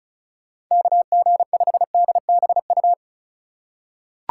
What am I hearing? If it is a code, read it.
Morse code, heard as a keyed tone, about 36 wpm, 700 Hz. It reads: KG5DBU
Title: USA Call Signs - 6 Characters - 2x Character Spacing - 35wpm